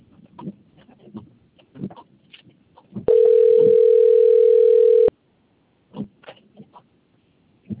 Ringback tone (North America), 440 + 480 Hz.
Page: Limeira IV